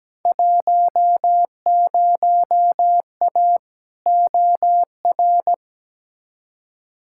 Morse code, heard as a keyed tone, about 17 wpm, 700 Hz. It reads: 10A OR
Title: Field Day Exchanges - 17wpm